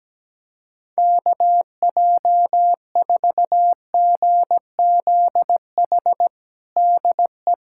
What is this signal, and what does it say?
Morse code, heard as a keyed tone, about 17 wpm, 700 Hz. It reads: KJ4GZH DE
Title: USA Callsign DE Callsign - 6 Characters - 17wpm